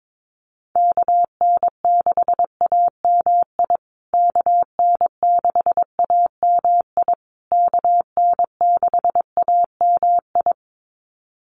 Morse code, heard as a keyed tone, about 22 wpm, 700 Hz. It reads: KN6AMI KN6AMI KN6AMI
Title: USA Call Signs Repeated 3x 22wpm